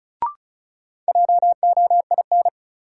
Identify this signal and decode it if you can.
Morse code, heard as a keyed tone, about 34 wpm, 700 Hz. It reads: JOIN
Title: Sentences from Top 1400 Words - New Word Drill Alt - 35wpm